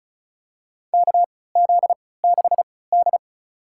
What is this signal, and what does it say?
Morse code, heard as a keyed tone, about 36 wpm, spaced as for 20 wpm, 700 Hz. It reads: KZ6D
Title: USA Call Signs - 4 Characters - 3x Character Spacing - 35wpm